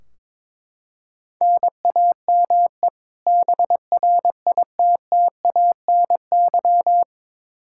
Morse code, heard as a keyed tone, about 22 wpm, 700 Hz. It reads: NAME BRITTANY
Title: K1USN SST QSO Practice 22wpm